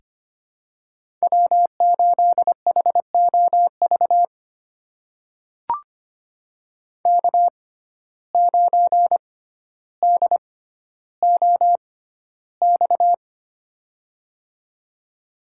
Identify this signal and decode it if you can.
Morse code, heard as a keyed tone, about 25 wpm, spaced as for 13 wpm, 700 Hz. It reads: W8HOV K9DOX
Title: USA Call Signs - 5 Characters - 6x Character Spacing - 25wpm